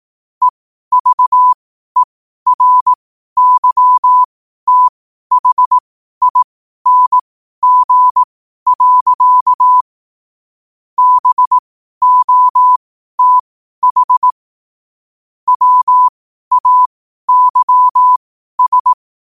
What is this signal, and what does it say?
Morse code, heard as a keyed tone, about 18 wpm, spaced as for 12 wpm, 1000 Hz. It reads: EVERYTHING. BOTH WAYS
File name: qotdfiles / 12wpm